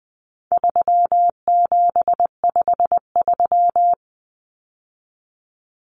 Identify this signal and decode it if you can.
Morse code, heard as a keyed tone, about 20 wpm, 700 Hz. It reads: 3753
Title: Common Cut Numbers - 4-Digit 20wpm